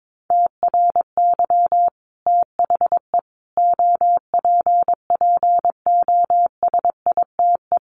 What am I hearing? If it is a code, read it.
Morse code, heard as a keyed tone, about 22 wpm, 700 Hz. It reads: TRY THE OPPOSITE
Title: Sentences from Top 1300 Words 22wpm